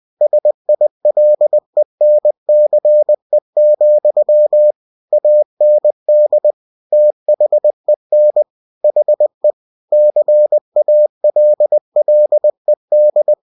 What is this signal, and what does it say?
Morse code, heard as a keyed tone, about 20 wpm, 600 Hz. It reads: SILENCE, AND THEN HE CALLED
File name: cw-books / winnie-the-pooh / winnie-20wpm-page-0007